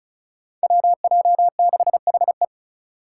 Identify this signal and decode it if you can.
Morse code, heard as a keyed tone, about 34 wpm, 700 Hz. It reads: WJ6HE
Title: Five Challenging Letter-Number Combinations 35wpm